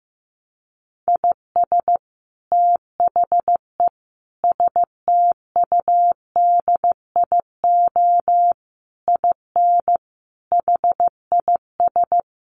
Morse code, heard as a keyed tone, about 15 wpm, 700 Hz. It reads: IS THE STUDIO IN HIS
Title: Sentences from Top 1200 Words 15wpm